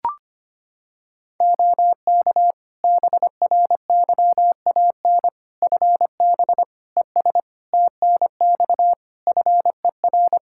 Morse code, heard as a keyed tone, about 25 wpm, 700 Hz. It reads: OK BRYAN FB ES TNX FER
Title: Practice QSO - Exchange 2 - Info - 25wpm